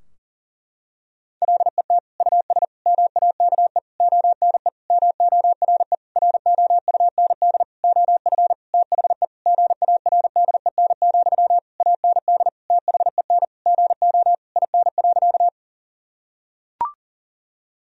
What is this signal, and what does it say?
Morse code, heard as a keyed tone, about 37 wpm, 700 Hz. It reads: LET US MAKE ONE MORE ROUND OF THE GARDEN, AND THEN GO IN.
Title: Sentences from Top 1000 Words - Review Part 4 - 40wpm